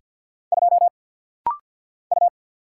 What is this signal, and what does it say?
Morse code, heard as a keyed tone, about 49 wpm, 700 Hz. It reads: 2 U
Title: Single Letter-Number - Rapid-Fire 50wpm